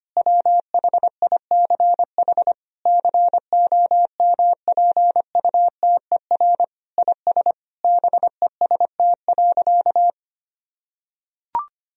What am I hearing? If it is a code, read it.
Morse code, heard as a keyed tone, about 25 wpm, 700 Hz. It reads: WHICH COMPUTER IS BEST.
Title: Sentences from Top 1000 Words - Review Part 3 - 25wpm